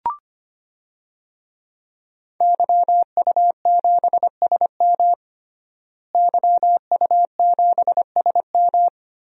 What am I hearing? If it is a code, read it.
Morse code, heard as a keyed tone, about 25 wpm, 700 Hz. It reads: YU7SM YU7SM
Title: DXCC-100 Call Signs - 5 Characters - Repeated 3x - 25wpm